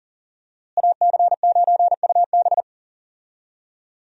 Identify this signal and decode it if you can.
Morse code, heard as a keyed tone, about 40 wpm, 700 Hz. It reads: AC9UB
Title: USA Call Signs - 5 Characters - Repeated 2x - 40wpm